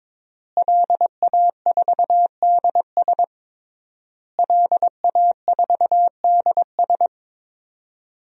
Morse code, heard as a keyed tone, about 22 wpm, 700 Hz. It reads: LA4DS LA4DS
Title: DXCC-100 Call Signs - 5 Characters - Repeated 3x - 22wpm